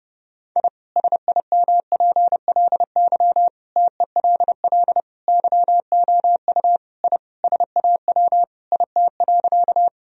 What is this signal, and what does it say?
Morse code, heard as a keyed tone, about 32 wpm, 700 Hz. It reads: I SIMPLY TELL YOU I SAW IT.
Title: Sentences from Top 1000 Words - Review Part 4 - 30wpm